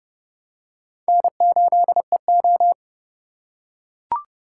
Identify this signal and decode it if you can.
Morse code, heard as a keyed tone, about 30 wpm, 700 Hz. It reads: N8EO
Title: USA Call Signs - 4 Characters - 3x Character Spacing - 30wpm